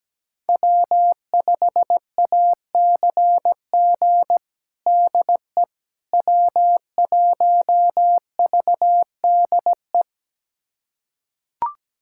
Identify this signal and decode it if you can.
Morse code, heard as a keyed tone, about 17 wpm, 700 Hz. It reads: W5ACG DE W1VDE
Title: USA Callsign DE Callsign - 5 Characters - 17wpm